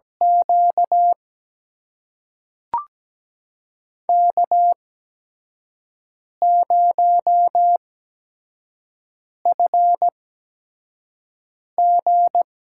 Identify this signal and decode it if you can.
Morse code, heard as a keyed tone, about 17 wpm, spaced as for 5 wpm, 700 Hz. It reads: Q K0FG
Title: USA Call Signs - 8x Character Spacing - 17wpm